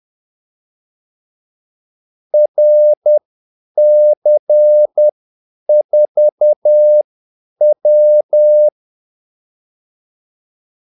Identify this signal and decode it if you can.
Morse code, heard as a keyed tone, about 10 wpm, spaced as for 8 wpm, 600 Hz. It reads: RC4W
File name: Callsigns-10-8wpm